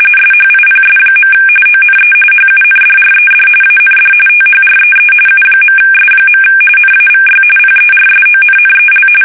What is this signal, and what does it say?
RTTY, about 45.45 baud. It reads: SHIPS. ONE HUNDRED AND THREE U.S. NAVY PERSONNEL
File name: fsk850 / usnavy5